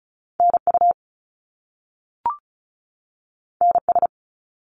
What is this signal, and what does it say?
Morse code, heard as a keyed tone, about 36 wpm, 700 Hz. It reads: NU NS
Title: Canadian Provinces and Territories 35wpm